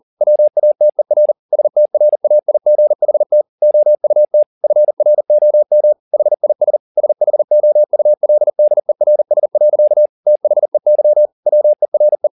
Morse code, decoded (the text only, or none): WATER STRAIGHT OUT FROM HIS SHOULDERS. THEY WERE